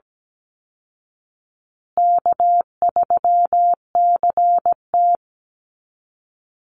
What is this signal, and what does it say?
Morse code, heard as a keyed tone, about 17 wpm, 700 Hz. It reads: K3CT
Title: Call Signs Repeated 2x - 17wpm